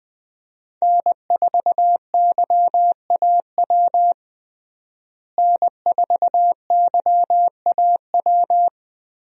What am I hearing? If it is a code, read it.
Morse code, heard as a keyed tone, about 20 wpm, 700 Hz. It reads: N4YAW N4YAW
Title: USA Call Signs - 5 Characters - Repeated 3x - 20wpm